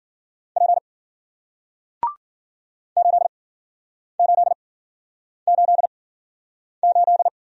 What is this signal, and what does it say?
Morse code, heard as a keyed tone, about 53 wpm, spaced as for 9 wpm, 700 Hz. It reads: P 8888